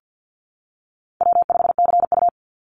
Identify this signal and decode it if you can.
Morse code, heard as a keyed tone, about 50 wpm, 700 Hz. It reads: W5CU
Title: Easy Call Signs - 50wpm